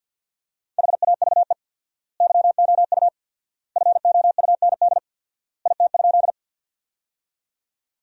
Morse code, heard as a keyed tone, about 50 wpm, 700 Hz. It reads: HAVE YOU FOUND IT?
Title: 4x Word-Spacing - Sentences from Top 300 Words - 50wpm